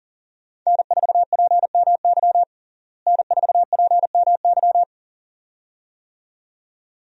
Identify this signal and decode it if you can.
Morse code, heard as a keyed tone, about 40 wpm, 700 Hz. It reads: N4PMY N4PMY
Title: USA Call Signs - 5 Characters - Repeated 2x - 40wpm